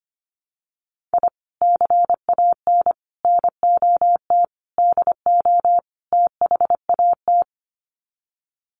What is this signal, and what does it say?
Morse code, heard as a keyed tone, about 25 wpm, 700 Hz. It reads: I CAN NOT DO THAT
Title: Sets of 5 Words Encore2 25wpm